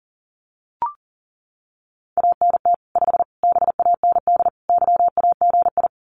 Morse code, heard as a keyed tone, about 40 wpm, 700 Hz. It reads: ANT 5 BAND YAGI
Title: Common QSO Phrases 40wpm